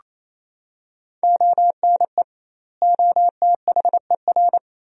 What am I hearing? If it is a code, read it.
Morse code, heard as a keyed tone, about 28 wpm, 700 Hz. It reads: ONE OTHER
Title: 2x Word-Spacing - Sentences from Top 400 Words - 28wpm